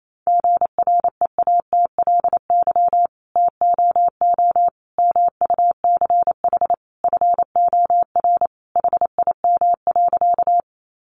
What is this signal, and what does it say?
Morse code, heard as a keyed tone, about 28 wpm, 700 Hz. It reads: GREATLY TOO MUCH FOR HIM.
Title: Sentences from Top 1400 Words 28wpm